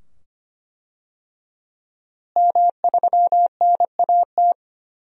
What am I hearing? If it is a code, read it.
Morse code, heard as a keyed tone, about 25 wpm, 700 Hz. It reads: M3NAT
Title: DXCC-100 Call Signs - 5 Characters - Repeated 2x - 25wpm